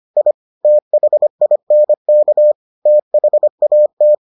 Morse code, heard as a keyed tone, about 25 wpm, 600 Hz. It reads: I THINK THAT